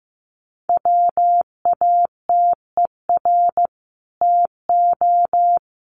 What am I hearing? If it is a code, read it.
Morse code, heard as a keyed tone, about 15 wpm, 700 Hz. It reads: WATER TO
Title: Sentences from Top 1500 Words 15wpm